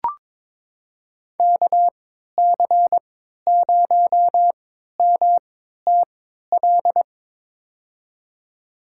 Morse code, heard as a keyed tone, about 22 wpm, spaced as for 13 wpm, 700 Hz. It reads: KC0MTL
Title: USA Call Signs - 6 Characters - 3x Character Spacing - 22wpm